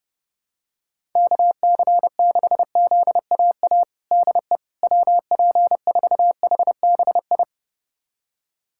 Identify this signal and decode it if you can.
Morse code, heard as a keyed tone, about 30 wpm, 700 Hz. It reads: KC6ZAA DE WP4HBI
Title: USA Callsign DE Callsign - 6 Characters - 30wpm